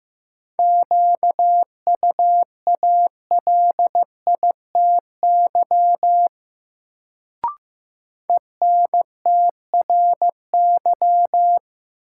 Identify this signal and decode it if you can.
Morse code, heard as a keyed tone, about 15 wpm, 700 Hz. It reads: QUALITY ENTRY
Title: Sentences from Top 1500 Words - New Word Drill Alt - 15wpm